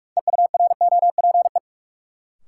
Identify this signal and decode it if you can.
Morse code, heard as a keyed tone, about 46 wpm, 700 Hz. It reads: EUROPE
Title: Sets of 5 Words Encore - New Word Drill - 45wpm